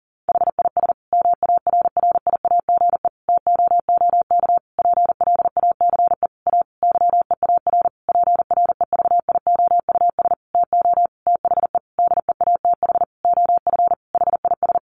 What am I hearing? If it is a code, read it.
Morse code, heard as a keyed tone, about 40 wpm, 700 Hz. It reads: HIS MARRIAGE TOOK PLACE A YEAR PREVIOUS TO THE DEATH OF HIS
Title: Sentences from Top 1500 Words 40wpm